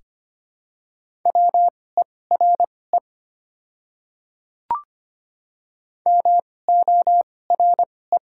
Morse code, heard as a keyed tone, about 25 wpm, spaced as for 18 wpm, 700 Hz. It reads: WERE MORE
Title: 2x Character Spacing - Top 300 Words - 25wpm